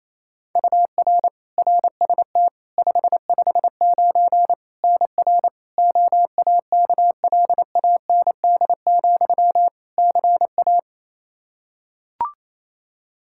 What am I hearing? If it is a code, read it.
Morse code, heard as a keyed tone, about 28 wpm, 700 Hz. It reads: UR RST 559 NR OAKLAND, CA
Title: K1USN SST QSO Practice 28wpm